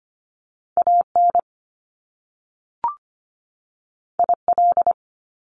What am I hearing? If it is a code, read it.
Morse code, heard as a keyed tone, about 25 wpm, 700 Hz. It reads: AN IL
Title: Top Two Letter Combinations in Words 25wpm